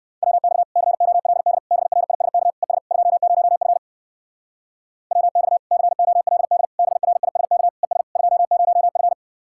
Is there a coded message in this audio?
Yes: Morse code from 0.2 to 9.1 s, heard as a keyed tone, about 69 wpm, 700 Hz. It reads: WX COLD BRISK ES 20F WX COLD BRISK ES 20F